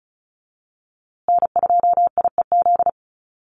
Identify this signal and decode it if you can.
Morse code, heard as a keyed tone, about 34 wpm, 700 Hz. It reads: N2IEZ
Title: USA Call Signs Repeated 3x 35wpm